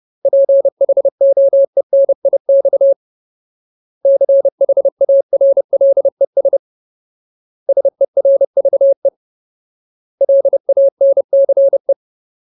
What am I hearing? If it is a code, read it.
Morse code, heard as a keyed tone, about 30 wpm, 550 Hz. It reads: PHOENIX CHARLES SERVE LANCE
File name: Ultimate Word List / 30WPM Ultimate